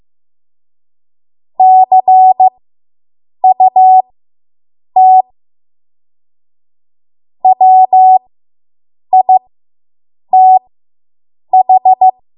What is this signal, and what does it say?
Morse code, heard as a keyed tone, about 15 wpm, spaced as for 7 wpm, 750 Hz. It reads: CUT WITH